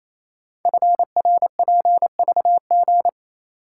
Morse code, heard as a keyed tone, about 27 wpm, 700 Hz. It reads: FRPVG
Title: Code Groups - 5 Letters - 28wpm